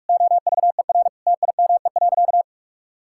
Morse code, heard as a keyed tone, about 46 wpm, 700 Hz. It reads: OVER TIME.
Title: Sentences from Top 1000 Words - Review Part 6 - 45wpm